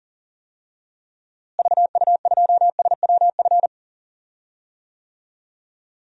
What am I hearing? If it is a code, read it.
Morse code, heard as a keyed tone, about 40 wpm, 700 Hz. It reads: VU2SWF
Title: DXCC-100 Call Signs - 6 Characters - Repeated 2x - 40wpm